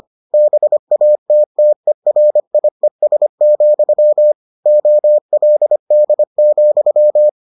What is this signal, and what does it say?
Morse code, heard as a keyed tone, about 25 wpm, 600 Hz. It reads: BATTERIES, OLD,